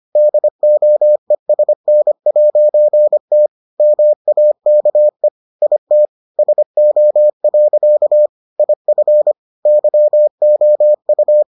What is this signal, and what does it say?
Morse code, heard as a keyed tone, about 25 wpm, 600 Hz. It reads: DOESN'T MAKE IT SO. IF YOU